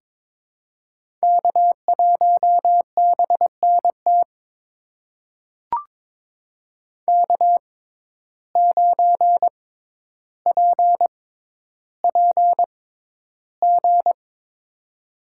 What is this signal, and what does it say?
Morse code, heard as a keyed tone, about 22 wpm, spaced as for 11 wpm, 700 Hz. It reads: K1BNT K9PPG
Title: USA Call Signs - 5 Characters - 6x Character Spacing - 22wpm